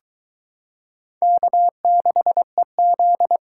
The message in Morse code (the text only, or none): K6EZ